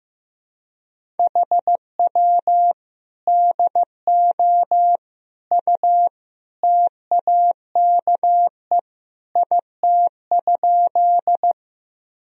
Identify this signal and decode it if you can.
Morse code, heard as a keyed tone, about 15 wpm, 700 Hz. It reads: HW DO U TAKE IT?
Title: Sentences from Top 200 Words and Abbreviations 15wpm